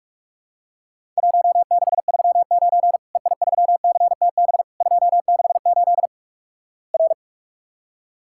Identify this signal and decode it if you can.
Morse code, heard as a keyed tone, about 46 wpm, 700 Hz. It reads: 1639 EI3CTB 268 R
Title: Worked All Europe DX Contest - QTC Exchange - 45wpm